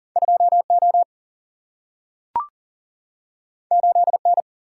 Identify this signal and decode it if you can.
Morse code, heard as a keyed tone, about 39 wpm, 700 Hz. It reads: 2O 8N
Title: Single-Letter Single-Number 40wpm